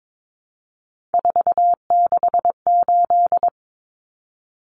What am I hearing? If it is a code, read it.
Morse code, heard as a keyed tone, about 22 wpm, 700 Hz. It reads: 468
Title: Numbers - 3-Digit 22wpm